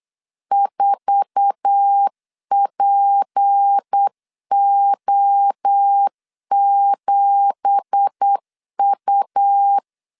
Morse code, heard as a keyed tone, about 8 wpm, 800 Hz. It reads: 4PO7U
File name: 8wpm